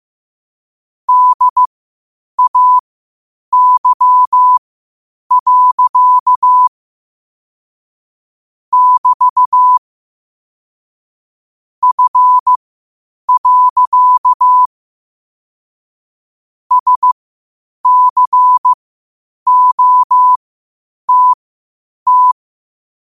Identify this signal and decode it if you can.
Morse code, heard as a keyed tone, about 15 wpm, spaced as for 8 wpm, 1000 Hz. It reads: DAY. <BT> F. SCOTT
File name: qotdfiles / 8wpm